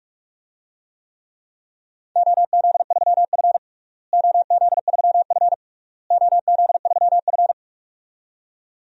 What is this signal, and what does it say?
Morse code, heard as a keyed tone, about 46 wpm, 700 Hz. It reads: OZ3F OZ3F OZ3F
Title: DXCC-100 Call Signs - 4 Characters - Repeated 3x - 45wpm